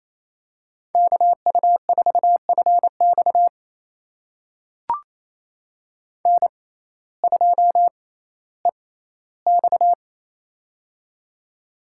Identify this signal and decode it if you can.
Morse code, heard as a keyed tone, about 28 wpm, spaced as for 15 wpm, 700 Hz. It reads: KU4FX N2EX
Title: USA Call Signs - 6x Character Spacing - 28wpm